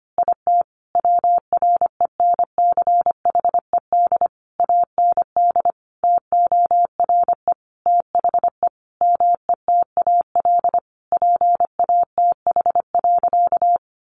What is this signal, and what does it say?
Morse code, heard as a keyed tone, about 25 wpm, 700 Hz. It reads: IT WRENCHED AND TORE THE METAL PATH.
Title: A Sound Of Thunder 25wpm